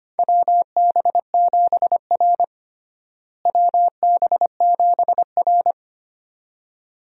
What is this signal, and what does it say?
Morse code, heard as a keyed tone, about 25 wpm, 700 Hz. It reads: WB7R WB7R
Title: USA Call Signs - 4 Characters - Repeated 2x - 25wpm